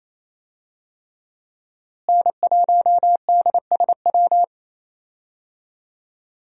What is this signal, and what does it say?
Morse code, heard as a keyed tone, about 27 wpm, 700 Hz. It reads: N1DSW
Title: USA Call Signs - 5 Characters - Repeated 3x - 28wpm